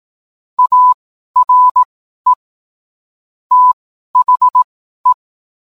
Morse code, heard as a keyed tone, about 18 wpm, spaced as for 12 wpm, 1000 Hz. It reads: ARE THE